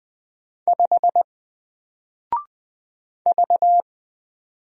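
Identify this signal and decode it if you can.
Morse code, heard as a keyed tone, about 20 wpm, 700 Hz. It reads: 5 V
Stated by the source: Single letter and number Morse drill at 20 wpm, from N4LCD.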